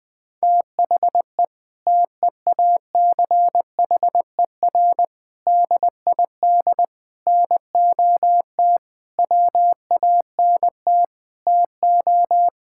Morse code, heard as a keyed tone, about 21 wpm, 700 Hz. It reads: THE TEACHER DID NOT WANT TO
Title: Sentences from Top 1000 Words - Review Part 6 - 20wpm